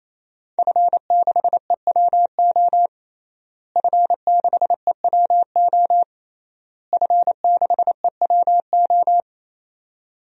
Morse code, heard as a keyed tone, about 28 wpm, 700 Hz. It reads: F6EWO F6EWO F6EWO
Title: DXCC-100 Call Signs - 5 Characters - Repeated 3x - 28wpm